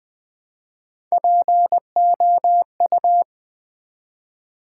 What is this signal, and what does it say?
Morse code, heard as a keyed tone, about 20 wpm, 700 Hz. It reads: POU
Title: Build A Word - 1 Letter At A Time - 3 to 5-Letter Words - 20wpm